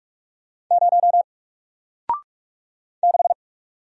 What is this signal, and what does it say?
Morse code, heard as a keyed tone, about 46 wpm, 700 Hz. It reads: 0 6
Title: Single Letter-Number 45wpm